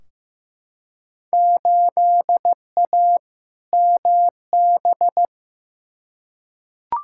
Morse code, heard as a keyed tone, about 15 wpm, 700 Hz. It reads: 8A MB
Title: Field Day Exchanges Encore - 15wpm